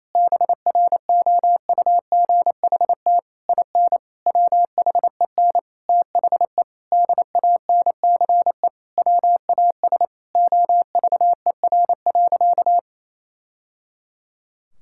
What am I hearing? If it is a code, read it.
Morse code, heard as a keyed tone, about 27 wpm, 700 Hz. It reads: BROUGHT IN WHEN THE DANCE WAS OVER.
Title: Sentences from Top 1000 Words - Review Part 6 - 28wpm